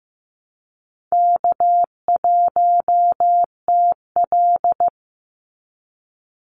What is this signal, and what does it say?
Morse code, heard as a keyed tone, about 15 wpm, 700 Hz. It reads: K1TL
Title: Easy Call Signs - 15wpm